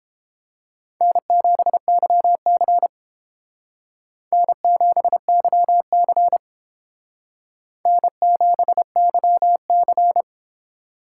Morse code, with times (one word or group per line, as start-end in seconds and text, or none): 1.0-2.9 N7YC
4.3-6.4 N7YC
7.9-10.2 N7YC